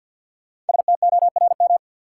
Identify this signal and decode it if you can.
Morse code, heard as a keyed tone, about 50 wpm, 700 Hz. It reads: STORM
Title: Sentences from Top 1500 Words - New Word Drill - 50wpm